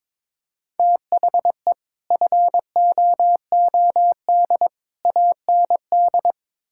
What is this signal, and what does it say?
Morse code, heard as a keyed tone, about 22 wpm, 700 Hz. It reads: THE FOOD AND
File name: practice / 22wpm / Build A Phrase - 5-Word Phrases - 22wpm